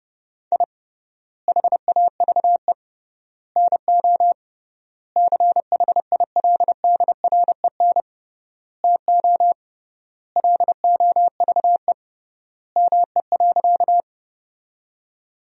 Morse code, heard as a keyed tone, about 30 wpm, 700 Hz. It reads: I HAVE NO CHILDREN TO LOVE ME.
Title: 3x Word-Spacing - Sentences from Top 400 Words - 30wpm